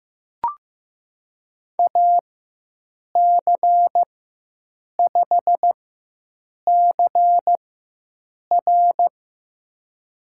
Morse code, heard as a keyed tone, about 15 wpm, spaced as for 7 wpm, 700 Hz. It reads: AC5CR
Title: USA Call Signs - 5 Characters - 4x Character Spacing - 15wpm